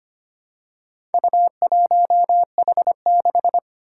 Morse code, heard as a keyed tone, about 25 wpm, 700 Hz. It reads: U1H6
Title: Four Challenging Letter-Number Combinations 25wpm